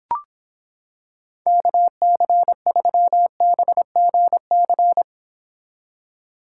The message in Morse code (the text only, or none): KC3BGC